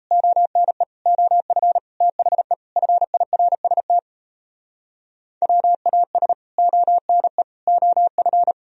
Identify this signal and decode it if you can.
Morse code, heard as a keyed tone, about 37 wpm, 700 Hz. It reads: ONE OF THE FIRST WAS ONE OF